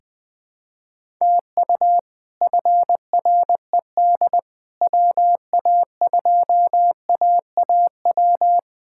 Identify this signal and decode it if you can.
Morse code, heard as a keyed tone, about 20 wpm, 700 Hz. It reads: TU FRED WA2AAW
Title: K1USN SST QSO Practice - 20wpm